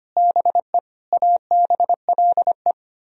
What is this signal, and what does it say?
Morse code, heard as a keyed tone, about 31 wpm, 700 Hz. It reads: BE ABLE